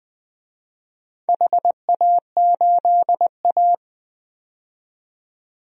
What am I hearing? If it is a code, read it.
Morse code, heard as a keyed tone, about 21 wpm, 700 Hz. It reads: HA8A
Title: DXCC-100 Call Signs - 4 Characters - 20wpm